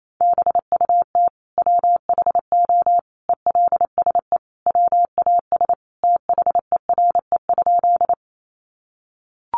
Morse code, heard as a keyed tone, about 28 wpm, 700 Hz. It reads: BUT WHO ELSE WAS THERE?
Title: Sentences from Top 800 Words 28wpm